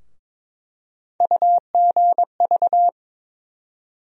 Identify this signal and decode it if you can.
Morse code, heard as a keyed tone, about 22 wpm, 700 Hz. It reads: UGV
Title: USA Call Sign Postfixes - 3 Letters 22wpm